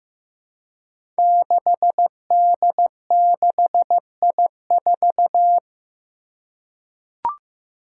Morse code, heard as a keyed tone, about 15 wpm, 700 Hz. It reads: 6D6I4
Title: Five Challenging Letter-Number Combinations 15wpm